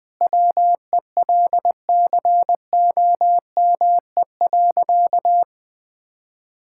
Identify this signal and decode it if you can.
Morse code, heard as a keyed tone, about 21 wpm, 700 Hz. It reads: WELCOME.
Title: Sentences with 7-Word Phrases - 20wpm